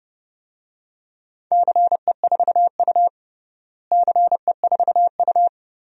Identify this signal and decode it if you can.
Morse code, heard as a keyed tone, about 30 wpm, 700 Hz. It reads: CE4U CE4U
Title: DXCC-100 Call Signs - 4 Characters - Repeated 2x - 30wpm